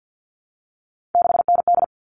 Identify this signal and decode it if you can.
Morse code, heard as a keyed tone, about 48 wpm, 700 Hz. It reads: 6ND
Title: All Cut Numbers - 3-Digit 50wpm